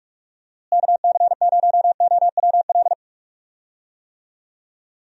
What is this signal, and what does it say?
Morse code, heard as a keyed tone, about 46 wpm, 700 Hz. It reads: KC0OWL
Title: USA Call Signs - 6 Characters - Repeated 3x - 45wpm